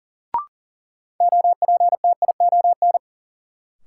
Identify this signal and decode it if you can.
Morse code, heard as a keyed tone, about 40 wpm, 700 Hz. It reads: OPTION
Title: Sentences from Top 1300 Words - New Word Drill - 40wpm